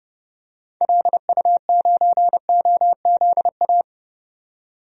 Morse code, heard as a keyed tone, about 30 wpm, 700 Hz. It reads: LU9OZA
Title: DXCC-100 Call Signs - 6 Characters - 30wpm